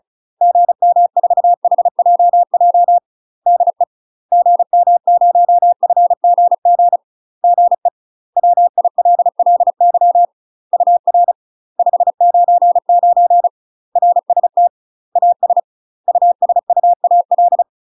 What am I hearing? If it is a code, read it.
Morse code, heard as a keyed tone, about 34 wpm, 700 Hz. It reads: GM4HJJ DE GM0FGG GE WILLY UR 599 RST AS USUAL
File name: UK MASTER QSOs TRAINING / 35WPM / UK MASTER QSOs 0043